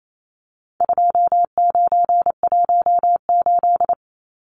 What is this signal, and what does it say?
Morse code, heard as a keyed tone, about 28 wpm, 700 Hz. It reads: 2918
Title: Numbers - 4-Digit 28wpm